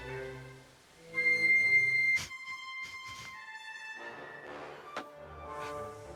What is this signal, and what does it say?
Dial-up modem handshake.